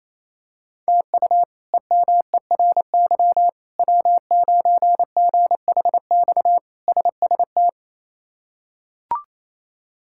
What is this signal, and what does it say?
Morse code, heard as a keyed tone, about 28 wpm, 700 Hz. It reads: TU EMERY W9GHX SST
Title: K1USN SST QSO Practice - 28wpm